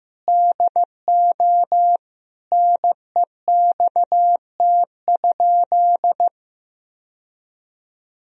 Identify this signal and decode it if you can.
Morse code, heard as a keyed tone, about 15 wpm, 700 Hz. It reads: DO NEXT?
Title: Sentences from Top 1000 Words - Review Part 1 - 15wpm